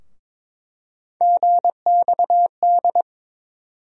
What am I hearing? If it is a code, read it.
Morse code, heard as a keyed tone, about 22 wpm, 700 Hz. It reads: GXD
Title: USA Call Sign Postfixes - 3 Letters 22wpm